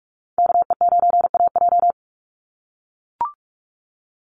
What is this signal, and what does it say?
Morse code, heard as a keyed tone, about 44 wpm, 700 Hz. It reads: KE9AJ
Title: NA SOTA Chaser Callsigns 45wpm